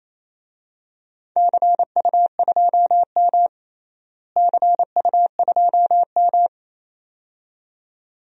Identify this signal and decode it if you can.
Morse code, heard as a keyed tone, about 28 wpm, 700 Hz. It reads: CU2M CU2M
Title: DXCC-100 Call Signs - 4 Characters - Repeated 2x - 28wpm